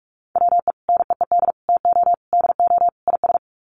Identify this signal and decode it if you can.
Morse code, heard as a keyed tone, about 44 wpm, 700 Hz. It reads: WE NEED TO DO IS
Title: Sets of 5 Words Encore2 45wpm